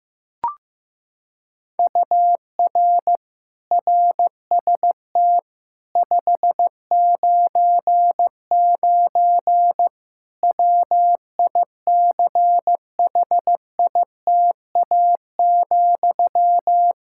Morse code, heard as a keyed tone, about 15 wpm, 700 Hz. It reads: UR RST 599 WICHITA,
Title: Practice QSO - Exchange 1 - Beginning - 15wpm